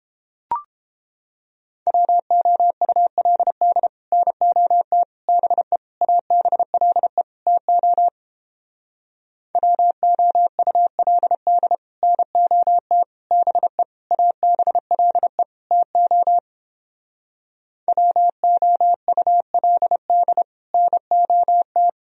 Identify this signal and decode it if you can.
Morse code, heard as a keyed tone, about 29 wpm, 700 Hz. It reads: WOULD NOT BE ABLE TO WOULD NOT BE ABLE TO WOULD NOT